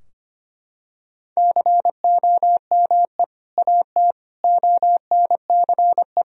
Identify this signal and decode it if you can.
Morse code, heard as a keyed tone, about 25 wpm, 700 Hz. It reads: COME AT ONCE
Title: Sentences from Top 1000 Words - Review Part 2 - 25wpm